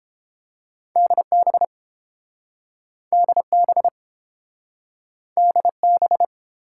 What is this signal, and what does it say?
Morse code, heard as a keyed tone, about 30 wpm, 700 Hz. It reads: DB DB DB